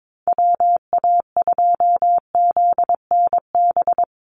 Morse code, heard as a keyed tone, about 22 wpm, 700 Hz. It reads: WA2ZNB
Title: USA Call Signs 22wpm